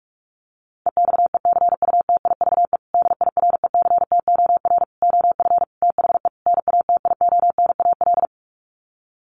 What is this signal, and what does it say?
Morse code, heard as a keyed tone, about 46 wpm, 700 Hz. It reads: EXECUTIVE DIRECTOR OF THE NATIONAL
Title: Sets of 5 Words Encore2 45wpm